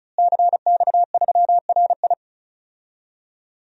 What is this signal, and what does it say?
Morse code, heard as a keyed tone, about 36 wpm, 700 Hz. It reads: CX3RI
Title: DXCC-100 Call Signs - 5 Characters - Repeated 3x - 35wpm